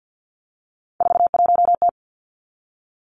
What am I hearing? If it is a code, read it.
Morse code, heard as a keyed tone, about 50 wpm, 700 Hz. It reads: 41T
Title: Three Challenging Letter-Number Combinations 50wpm